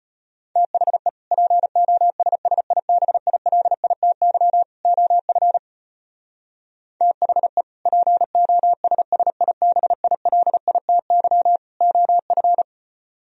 Morse code, heard as a keyed tone, about 33 wpm, 700 Hz. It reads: THE POSSIBILITY OF THE POSSIBILITY OF